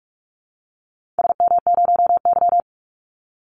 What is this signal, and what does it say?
Morse code, heard as a keyed tone, about 46 wpm, 700 Hz. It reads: SM0Y
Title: Call Signs - 45wpm